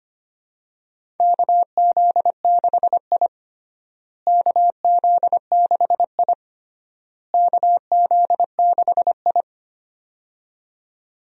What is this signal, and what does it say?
Morse code, heard as a keyed tone, about 25 wpm, 700 Hz. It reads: KZ6I KZ6I KZ6I
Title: USA Call Signs - 4 Characters - Repeated 3x - 25wpm